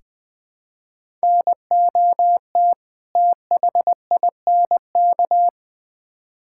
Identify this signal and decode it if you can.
Morse code, heard as a keyed tone, about 20 wpm, 700 Hz. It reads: NOT THINK
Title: Build A Phrase - 2-Word Phrases - 20wpm